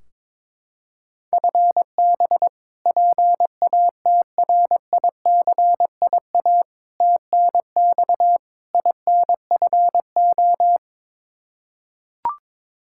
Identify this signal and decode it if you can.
Morse code, heard as a keyed tone, about 22 wpm, 700 Hz. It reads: FB PATRICIA TNX INFO
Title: Practice QSO - Exchange 3 - More Info - 22wpm